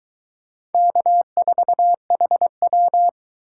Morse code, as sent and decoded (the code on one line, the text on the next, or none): -.- ....- .... .--
K4HW